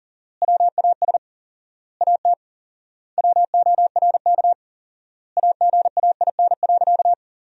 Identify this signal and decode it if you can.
Morse code, heard as a keyed tone, about 40 wpm, 700 Hz. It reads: WAS AT WORK AGAIN.
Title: 4x Word-Spacing - Sentences from Top 200 Words - 40wpm